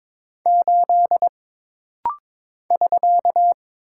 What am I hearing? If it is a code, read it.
Morse code, heard as a keyed tone, about 22 wpm, 700 Hz. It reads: 8 <SK>
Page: This rapid-fire timing is geared to push your ICR (Instant Character Recognition) skills.